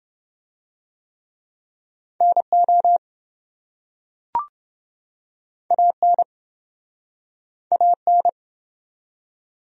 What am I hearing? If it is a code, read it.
Morse code, heard as a keyed tone, about 27 wpm, 700 Hz. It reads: NO AN AN